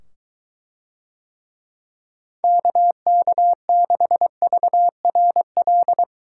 Morse code, heard as a keyed tone, about 21 wpm, 700 Hz. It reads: KK6VRL